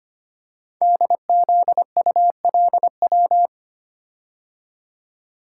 Morse code, heard as a keyed tone, about 25 wpm, 700 Hz. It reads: DZULW
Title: Code Groups - 5 Letters - 25wpm